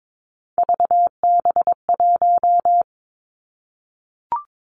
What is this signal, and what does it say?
Morse code, heard as a keyed tone, about 22 wpm, 700 Hz. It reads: VB1
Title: Three Challenging Letter-Number Combinations 22wpm